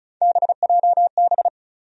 Morse code, heard as a keyed tone, about 34 wpm, 700 Hz. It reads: BJB
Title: USA Call Sign Postfixes - 3 Letters 35wpm